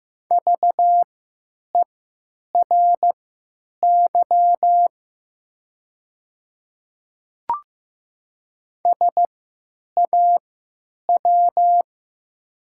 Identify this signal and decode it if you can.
Morse code, heard as a keyed tone, about 15 wpm, spaced as for 9 wpm, 700 Hz. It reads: VERY SAW
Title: 3x Character Spacing - Top 500 Words - 15wpm